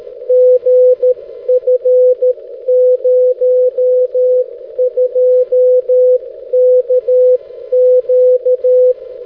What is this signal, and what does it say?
Morse code, heard as a keyed tone, about 13 wpm, 500 Hz. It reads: GF02KQ
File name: 6m-lu6ee-b